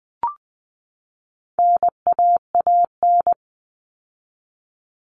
Morse code, heard as a keyed tone, about 20 wpm, 700 Hz. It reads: NAAN
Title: All Cut Numbers - 4-Digit 20wpm